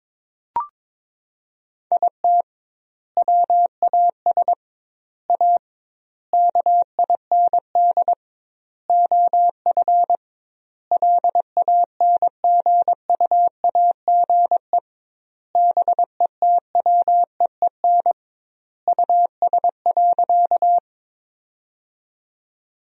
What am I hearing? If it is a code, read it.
Morse code, heard as a keyed tone, about 22 wpm, 700 Hz. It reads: IT WAS A KIND OF LANGUAGE BETWEEN US.
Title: 2x Word-Spacing - Sentences from Top 500 Words - 22wpm